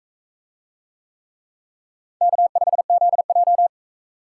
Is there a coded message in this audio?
Yes: Morse code from 2.2 to 3.7 s, heard as a keyed tone, about 42 wpm, 700 Hz. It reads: K5ZJ